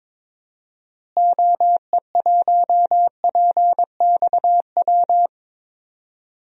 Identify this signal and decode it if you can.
Morse code, heard as a keyed tone, about 22 wpm, 700 Hz. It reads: OE1PXW
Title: DXCC-100 Call Signs - 6 Characters - Repeated 2x - 22wpm